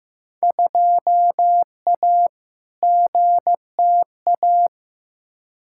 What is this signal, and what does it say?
Morse code, heard as a keyed tone, about 15 wpm, 700 Hz. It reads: 2A GTA
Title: Field Day Exchanges Encore - 15wpm